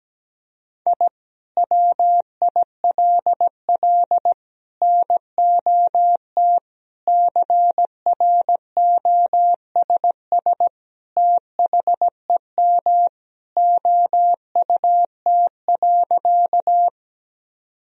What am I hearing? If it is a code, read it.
Morse code, heard as a keyed tone, about 17 wpm, 700 Hz. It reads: I WILL NOT CROSS THEM OUT.
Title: Sentences from Top 1000 Words - Review Part 1 - 17wpm